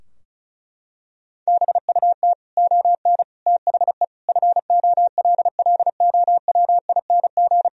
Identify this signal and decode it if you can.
Morse code, heard as a keyed tone, about 36 wpm, 700 Hz. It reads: BUT ON THE FOLLOWING
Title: Sentences from Top 1000 Words - Review Part 4 - 35wpm